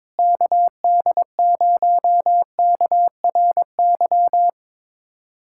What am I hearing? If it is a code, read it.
Morse code, heard as a keyed tone, about 21 wpm, 700 Hz. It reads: KD0KRY
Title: USA Callsign DE Callsign - 6 Characters - 22wpm